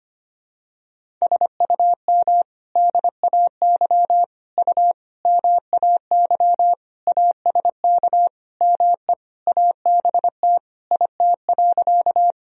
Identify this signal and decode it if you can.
Morse code, heard as a keyed tone, about 25 wpm, 700 Hz. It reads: SUM DAY U MAY ASK ME ABT IT.
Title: Sentences from Top 200 Words and Abbreviations 25wpm